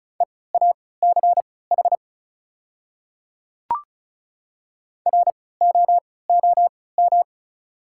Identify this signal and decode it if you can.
Morse code, heard as a keyed tone, about 34 wpm, spaced as for 20 wpm, 700 Hz. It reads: EACH ROOM
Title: 3x Character Spacing - Top 300 Words - 35wpm